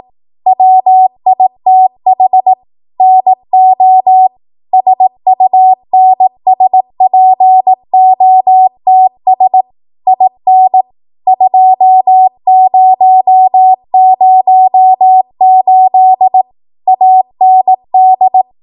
Morse code, heard as a keyed tone, about 18 wpm, 750 Hz. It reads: WITH NO SUNSPOTS IN 2008 AND
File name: Archive / 18 WPM / 18WPM